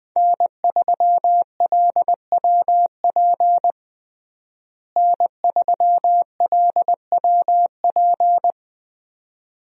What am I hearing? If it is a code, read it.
Morse code, heard as a keyed tone, about 20 wpm, 700 Hz. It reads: N3LWP N3LWP
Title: USA Call Signs - 5 Characters - Repeated 2x - 20wpm